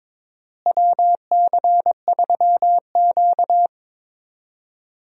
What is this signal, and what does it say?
Morse code, heard as a keyed tone, about 22 wpm, 700 Hz. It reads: WC3Q
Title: USA Call Signs - 4 Characters - Repeated 2x - 22wpm